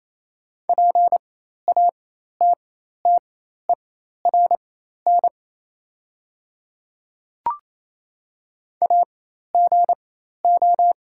Morse code, heard as a keyed tone, about 28 wpm, spaced as for 13 wpm, 700 Hz. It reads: PATTERN AGO
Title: 4x Character Spacing - Top 500 Words - 28wpm